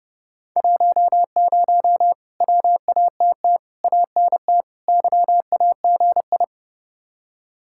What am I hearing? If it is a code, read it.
Morse code, heard as a keyed tone, about 30 wpm, 700 Hz. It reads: 10 WATT ANT YAGI
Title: Speed-Racing Common QSO Phrases Encore - 30wpm